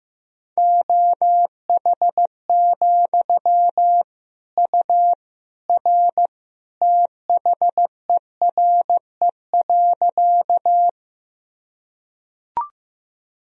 Morse code, heard as a keyed tone, about 15 wpm, 700 Hz. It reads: OH, U R THERE.
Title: Sentences from Top 500 Words and Abbreviations 15wpm